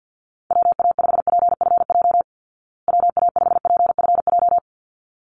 Morse code, heard as a keyed tone, about 50 wpm, 700 Hz. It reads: WA5PFJ WA5PFJ
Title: Call Signs Repeated 3x - 50wpm